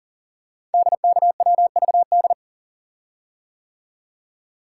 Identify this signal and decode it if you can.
Morse code, heard as a keyed tone, about 39 wpm, 690 Hz. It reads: DKWVD
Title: Code Groups - 5 Letters - 40wpm